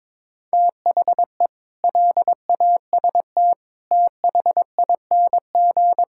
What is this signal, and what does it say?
Morse code, heard as a keyed tone, about 21 wpm, 700 Hz. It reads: THE LAST THING
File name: practice / 22wpm / Sets of 7 Words - 22wpm